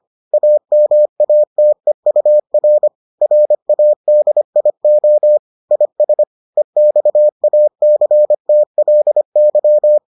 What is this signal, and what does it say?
Morse code, heard as a keyed tone, about 25 wpm, 600 Hz. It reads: AMATEUR RADIO IS EXACTLY